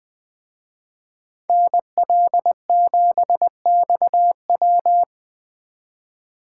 Morse code, heard as a keyed tone, about 24 wpm, 700 Hz. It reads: NL7XW
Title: Speed Racing - USA 2x2 Call Signs 20wpm